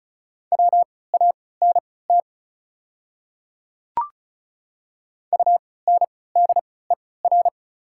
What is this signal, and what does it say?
Morse code, heard as a keyed tone, about 36 wpm, spaced as for 20 wpm, 700 Hz. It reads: WANT UNDER
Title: 3x Character Spacing - Top 200 Words - 35wpm